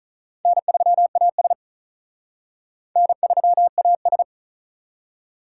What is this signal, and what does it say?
Morse code, heard as a keyed tone, about 42 wpm, 700 Hz. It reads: N3AS N3AS
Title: Speed Racing - USA 1x2 Call Signs 35wpm